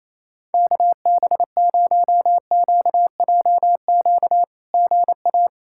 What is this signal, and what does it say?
Morse code, heard as a keyed tone, about 28 wpm, 690 Hz. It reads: KB0QJQ GA
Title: SOTA QSO Practice 28wpm